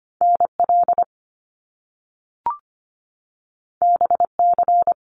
Morse code, heard as a keyed tone, about 25 wpm, 700 Hz. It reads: NL BC
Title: Canadian Provinces and Territories 25wpm